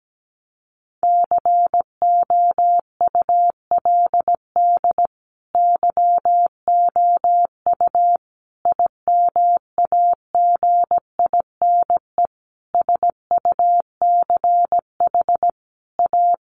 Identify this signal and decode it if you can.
Morse code, heard as a keyed tone, about 17 wpm, 700 Hz. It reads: COULD YOU IMAGINE SUCH A
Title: Sentences from Top 1400 Words 17wpm